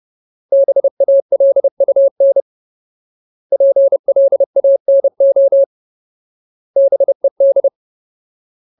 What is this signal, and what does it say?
Morse code, heard as a keyed tone, about 30 wpm, 550 Hz. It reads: BALUN PLANO BED
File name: Ultimate Word List / 30WPM Ultimate